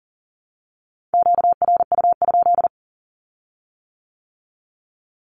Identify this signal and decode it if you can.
Morse code, heard as a keyed tone, about 38 wpm, 700 Hz. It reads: QRU?
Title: Q-Codes with Spoken Meaning - 40wpm